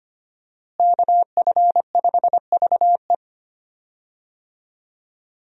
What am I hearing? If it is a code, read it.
Morse code, heard as a keyed tone, about 25 wpm, 690 Hz. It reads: KF5VE
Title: USA Call Signs - 5 Characters - Repeated 2x - 25wpm